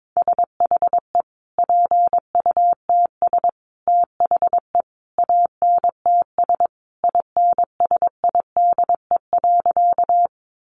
Morse code, heard as a keyed tone, about 22 wpm, 700 Hz. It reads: SHE PUTS THE ANTS INSIDE.
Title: Easy Phrases 22wpm